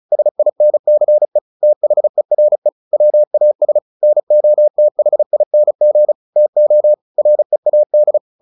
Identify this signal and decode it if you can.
Morse code, heard as a keyed tone, about 34 wpm, 600 Hz. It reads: SINCE THERE WAS NOTHING TO READ
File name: the old man and the sea 2 35wpm